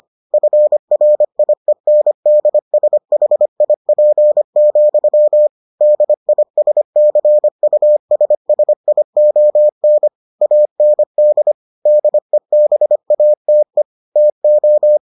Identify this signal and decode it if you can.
Morse code, heard as a keyed tone, about 25 wpm, 600 Hz. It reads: FRIENDSHIP, DISCUSSION AND DEBATE TO